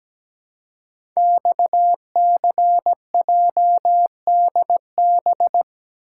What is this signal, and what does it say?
Morse code, heard as a keyed tone, about 17 wpm, 700 Hz. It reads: XCJDB
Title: Code Groups - 5 Letters - 17wpm